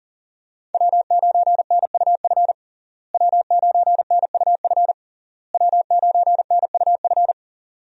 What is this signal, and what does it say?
Morse code, heard as a keyed tone, about 40 wpm, 700 Hz. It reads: W9NUF W9NUF W9NUF
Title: USA Call Signs - 5 Characters - Repeated 3x - 40wpm